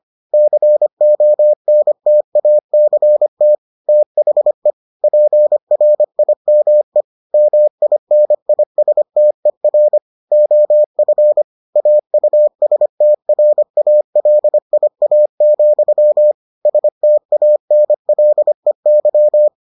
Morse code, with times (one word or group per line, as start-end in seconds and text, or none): 0.3-3.6 CONTACT
3.9-4.7 THE
5.0-7.0 PRIME
7.3-10.0 MINISTER
10.3-11.4 OF
11.8-16.3 AUSTRALIA,
16.7-19.5 STANLEY